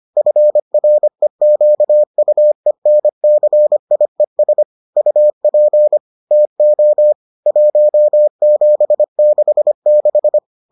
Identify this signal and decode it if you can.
Morse code, heard as a keyed tone, about 25 wpm, 600 Hz. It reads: FREQUENCIES UP TO 1766